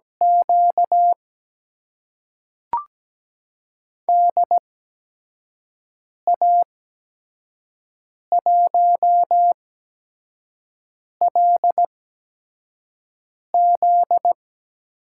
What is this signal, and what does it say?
Morse code, heard as a keyed tone, about 17 wpm, spaced as for 5 wpm, 700 Hz. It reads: Q DA1LZ
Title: USA Call Signs - 8x Character Spacing - 17wpm